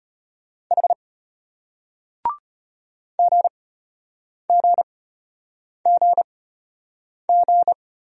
Morse code, heard as a keyed tone, about 34 wpm, spaced as for 8 wpm, 700 Hz. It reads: H GGGG